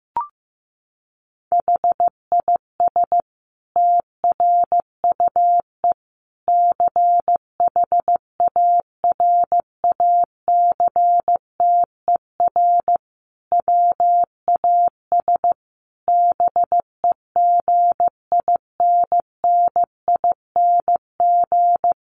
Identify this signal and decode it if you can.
Morse code, heard as a keyed tone, about 15 wpm, 700 Hz. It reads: HIS TRUE CHARACTER WAS BEGINNING
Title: Sentences from Top 1200 Words 15wpm